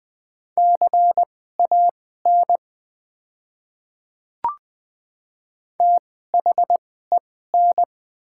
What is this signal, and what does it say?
Morse code, heard as a keyed tone, about 20 wpm, spaced as for 14 wpm, 700 Hz. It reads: CAN THEN
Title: 2x Character Spacing - Top 100 Words - 20wpm